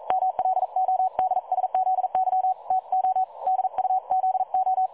Morse code, heard as a keyed tone, about 44 wpm, 750 Hz. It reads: MLODSZY TO DUZO
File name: dlazaawansowanych / nagraniazpasma / websdr recording 3565.0kHz